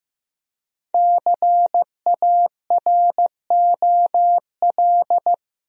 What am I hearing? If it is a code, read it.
Morse code, heard as a keyed tone, about 15 wpm, 700 Hz. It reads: CAROL
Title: USA Common Names - 15wpm